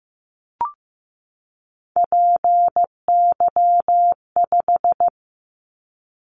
Morse code, heard as a keyed tone, about 15 wpm, 700 Hz. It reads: PY5
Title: Callsign Prefixes for Top 50 DXCC - 15wpm